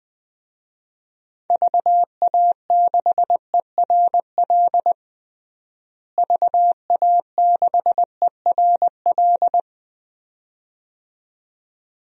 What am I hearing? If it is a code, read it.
Morse code, heard as a keyed tone, about 20 wpm, 700 Hz. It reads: VA6ERL VA6ERL
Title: DXCC-100 Call Signs - 6 Characters - Repeated 2x - 20wpm